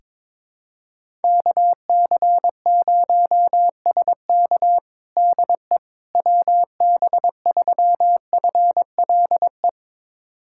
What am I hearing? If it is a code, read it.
Morse code, heard as a keyed tone, about 22 wpm, 700 Hz. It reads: KC0SK DE WB3FLE
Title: USA Callsign DE Callsign 22wpm